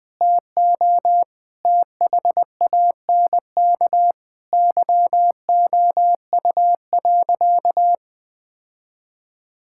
Morse code, heard as a keyed tone, about 21 wpm, 700 Hz. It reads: TO THANK YOU.
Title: Sentences with 8-Word Phrases - 20wpm